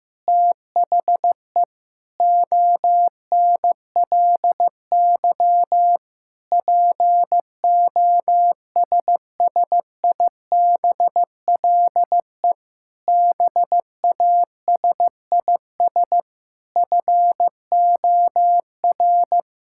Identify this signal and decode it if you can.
Morse code, heard as a keyed tone, about 15 wpm, 700 Hz. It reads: THE ONLY POSSIBLE BASIS FOR
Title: Sentences from Top 1000 Words - Review Part 5 - 15wpm